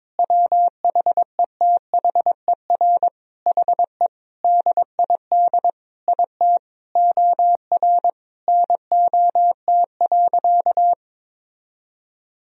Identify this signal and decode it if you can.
Morse code, heard as a keyed tone, about 22 wpm, 700 Hz. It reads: WHETHER HE DID IT OR NOT.
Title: Sentences from Top 1000 Words - Review Part 3 - 22wpm